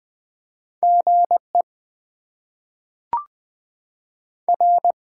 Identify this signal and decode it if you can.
Morse code, heard as a keyed tone, about 20 wpm, spaced as for 5 wpm, 700 Hz. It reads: GER
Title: Top 20 QSO Elements - 6x Character Spacing - 20wpm